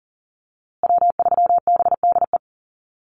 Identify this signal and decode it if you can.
Morse code, heard as a keyed tone, about 40 wpm, 700 Hz. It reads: W3BDE
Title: USA Call Signs Repeated 2x 40wpm